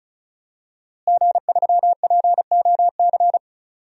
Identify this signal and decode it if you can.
Morse code, heard as a keyed tone, about 36 wpm, 700 Hz. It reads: G3POC
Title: DXCC-100 Call Signs - 5 Characters - Repeated 3x - 35wpm